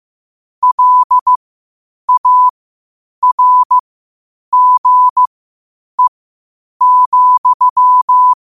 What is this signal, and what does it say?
Morse code, heard as a keyed tone, about 15 wpm, spaced as for 8 wpm, 1000 Hz. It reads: LARGE,
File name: qotdfiles / 8wpm